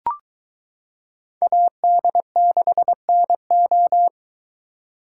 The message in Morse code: AD6NO

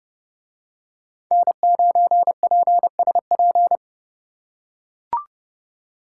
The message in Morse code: N9PSP

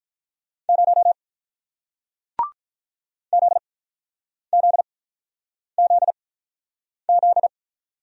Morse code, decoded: 0 Z Z Z Z

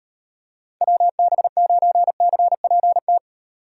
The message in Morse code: WB9CPT

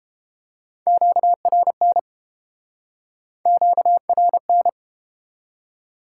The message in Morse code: QRN QRN

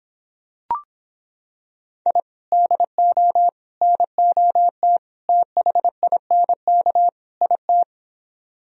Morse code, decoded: I DO NOT THINK IT

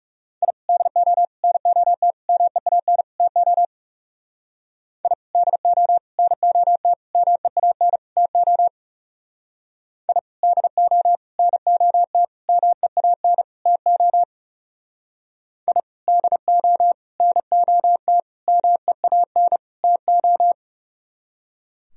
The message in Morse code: I DO NOT MEAN TO I DO NOT MEAN TO I DO NOT MEAN TO I DO NOT MEAN TO

